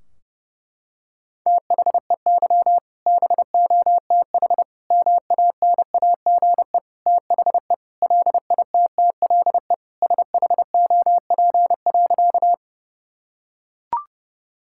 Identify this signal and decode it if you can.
Morse code, heard as a keyed tone, about 30 wpm, 700 Hz. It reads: THEY BOTH MANAGE THE LITTLE SHOP.
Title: Sentences from Top 1600 Words 30wpm